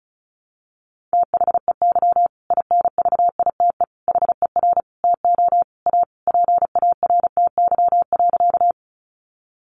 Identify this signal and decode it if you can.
Morse code, heard as a keyed tone, about 34 wpm, 700 Hz. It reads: THEY INVITE HER TO A PARTY.
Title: Easy Phrases 35wpm